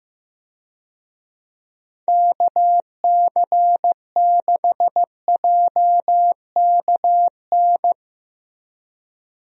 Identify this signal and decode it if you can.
Morse code, heard as a keyed tone, about 15 wpm, 700 Hz. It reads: KC6JKN
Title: USA Call Signs - 6 Characters - Repeated 2x - 15wpm